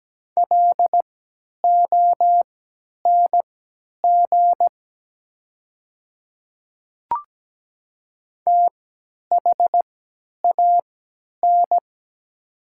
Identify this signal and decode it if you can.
Morse code, heard as a keyed tone, about 17 wpm, spaced as for 10 wpm, 690 Hz. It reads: LONG THAN
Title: 3x Character Spacing - Top 100 Words - 17wpm